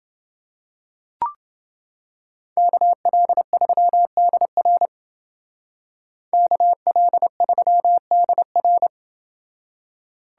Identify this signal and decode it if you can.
Morse code, heard as a keyed tone, about 30 wpm, 700 Hz. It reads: KL3DR KL3DR